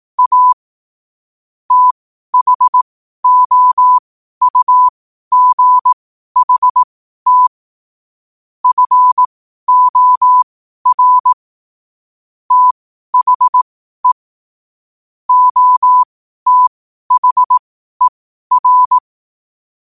Morse code, heard as a keyed tone, about 18 wpm, spaced as for 12 wpm, 1000 Hz. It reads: A THOUGHT FOR THE OTHER
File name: qotdfiles / 12wpm